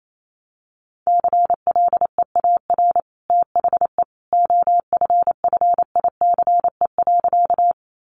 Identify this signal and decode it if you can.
Morse code, heard as a keyed tone, about 28 wpm, 700 Hz. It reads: CLEAR THE OFFICE.
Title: Sentences from Top 1000 Words 28wpm